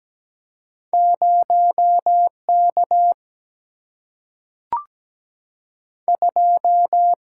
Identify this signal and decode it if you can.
Morse code, heard as a keyed tone, about 17 wpm, 700 Hz. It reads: 0 K 2
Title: Single-Letter Single-Number 17wpm